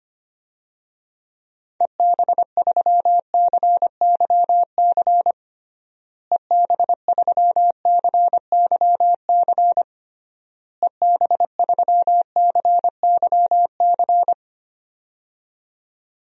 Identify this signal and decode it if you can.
Morse code, heard as a keyed tone, about 25 wpm, 700 Hz. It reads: EB3CYC EB3CYC EB3CYC
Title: DXCC-100 Call Signs - 6 Characters - Repeated 3x - 25wpm